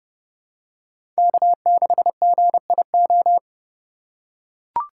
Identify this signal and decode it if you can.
Morse code, heard as a keyed tone, about 30 wpm, 700 Hz. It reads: K6GIO
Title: USA Call Signs - 5 Characters - 3x Character Spacing - 30wpm